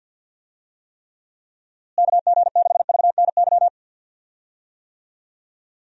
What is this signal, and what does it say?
Morse code, heard as a keyed tone, about 46 wpm, 700 Hz. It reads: KG6VNY
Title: USA Call Signs - 6 Characters - Repeated 3x - 50wpm